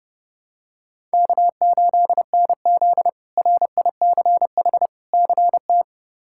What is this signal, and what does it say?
Morse code, heard as a keyed tone, about 29 wpm, 700 Hz. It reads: K8NZ RICH CT
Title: K1USN SST QSO Practice - 30wpm